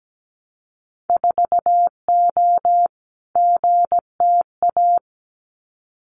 Morse code, heard as a keyed tone, about 17 wpm, 700 Hz. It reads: 4O GTA
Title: Winter Field Day Exchanges 17wpm